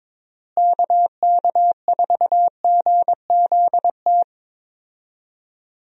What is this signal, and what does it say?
Morse code, heard as a keyed tone, about 22 wpm, 700 Hz. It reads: KK4GZT
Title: USA Call Signs - 6 Characters - Repeated 3x - 22wpm